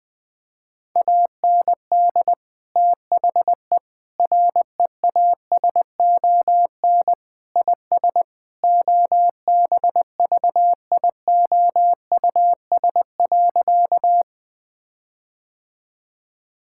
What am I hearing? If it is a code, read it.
Morse code, heard as a keyed tone, about 20 wpm, 700 Hz. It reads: AND THE REASON IS OBVIOUS.
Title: Sentences from Top 1600 Words 20wpm